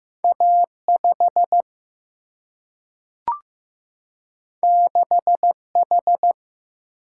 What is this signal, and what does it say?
Morse code, heard as a keyed tone, about 15 wpm, 700 Hz. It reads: A 5 6 H
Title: Single-Letter Single-Number 15wpm